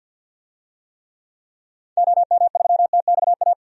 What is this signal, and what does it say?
Morse code, heard as a keyed tone, about 51 wpm, 700 Hz. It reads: OM3TXA
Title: DXCC-100 Call Signs - 6 Characters - Repeated 2x - 50wpm